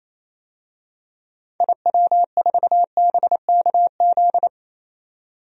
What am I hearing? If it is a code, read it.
Morse code, heard as a keyed tone, about 28 wpm, 700 Hz. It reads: IW4BKZ
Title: DXCC-100 Call Signs - 6 Characters - 28wpm